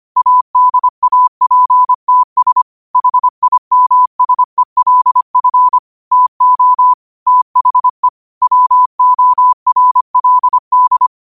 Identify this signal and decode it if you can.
Morse code, heard as a keyed tone, about 25 wpm, 1000 Hz. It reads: ADAPTS HIMSELF TO THE WORLD
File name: qotdfiles / 25wpm